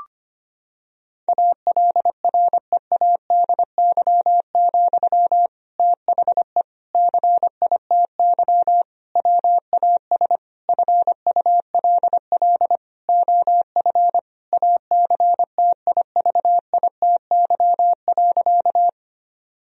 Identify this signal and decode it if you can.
Morse code, heard as a keyed tone, about 25 wpm, 700 Hz. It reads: ALREADY, THE CITY WAS FULL OF ACTIVITY.
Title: Sentences from Top 1000 Words - Review Part 4 - 25wpm